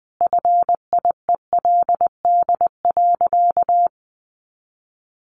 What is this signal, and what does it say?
Morse code, heard as a keyed tone, about 20 wpm, 700 Hz. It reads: FIELD.
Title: Easy Phrases 20wpm